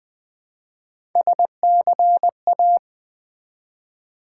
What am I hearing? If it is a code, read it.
Morse code, heard as a keyed tone, about 20 wpm, 700 Hz. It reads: SCA
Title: Build A Word - 1 Letter At A Time - 5-Letter Words - 20wpm